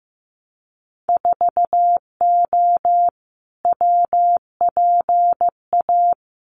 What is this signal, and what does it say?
Morse code, heard as a keyed tone, about 15 wpm, 700 Hz. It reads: 4O WPA
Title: Winter Field Day Exchanges 15wpm